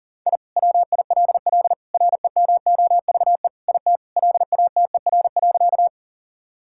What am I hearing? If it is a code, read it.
Morse code, heard as a keyed tone, about 40 wpm, 700 Hz. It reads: I WILL REMOVE IT LATER.
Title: Sentences from Top 1000 Words - Review Part 5 - 40wpm